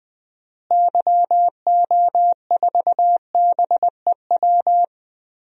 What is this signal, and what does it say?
Morse code, heard as a keyed tone, about 21 wpm, 700 Hz. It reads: YO4BEW
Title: DXCC-100 Call Signs - 6 Characters - Repeated 3x - 20wpm